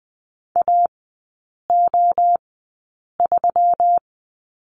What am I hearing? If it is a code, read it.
Morse code, heard as a keyed tone, about 20 wpm, spaced as for 8 wpm, 700 Hz. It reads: AO3
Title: 138 Lesson 23e - Letter f - Three Characters 20wpm